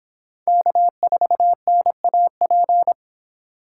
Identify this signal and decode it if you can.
Morse code, heard as a keyed tone, about 26 wpm, 700 Hz. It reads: K4NAP